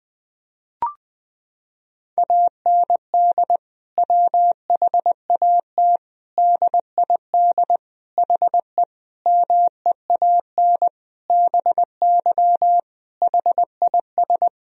Morse code, heard as a keyed tone, about 21 wpm, 700 Hz. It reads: AND WHAT DID HE MEAN BY HIS
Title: Sentences from Top 1000 Words - Review Part 3 - 20wpm